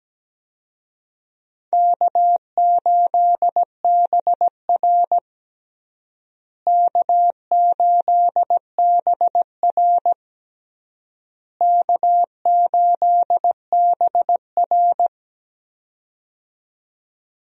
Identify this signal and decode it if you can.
Morse code, heard as a keyed tone, about 17 wpm, 700 Hz. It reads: K8BR K8BR K8BR
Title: USA Call Signs - 4 Characters - Repeated 3x - 17wpm